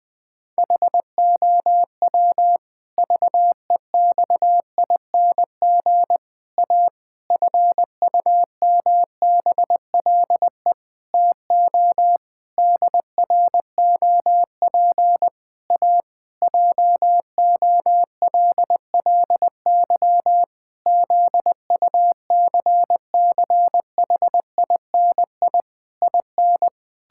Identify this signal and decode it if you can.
Morse code, heard as a keyed tone, about 20 wpm, 700 Hz. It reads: HOW VEXING A FUMBLE TO DROP A JOLLY ZUCCHINI IN
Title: Pangram Challenge v2 - 20wpm